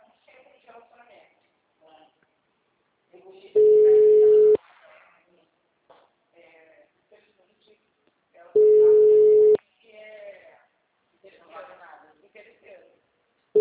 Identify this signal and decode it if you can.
Ringback tone (Europe, 425 Hz), with speech.
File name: out- -2051- - - .38627